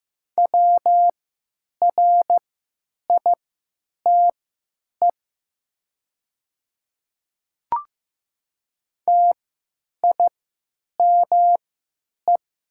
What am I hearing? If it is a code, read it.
Morse code, heard as a keyed tone, about 15 wpm, spaced as for 9 wpm, 700 Hz. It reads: WRITE TIME
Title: 3x Character Spacing - Top 100 Words - 15wpm